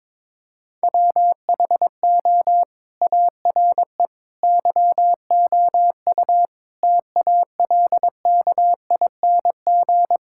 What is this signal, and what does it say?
Morse code, heard as a keyed tone, about 22 wpm, 700 Hz. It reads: WHO ARE YOU TALKING
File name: practice / 22wpm / Sentences with 2-Word Phrases - 22wpm